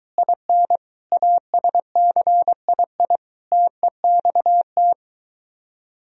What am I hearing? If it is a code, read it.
Morse code, heard as a keyed tone, about 22 wpm, 700 Hz. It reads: IN ASCII TEXT